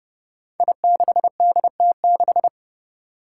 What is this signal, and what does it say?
Morse code, heard as a keyed tone, about 30 wpm, 700 Hz. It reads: I6DT6
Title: Five Challenging Letter-Number Combinations 30wpm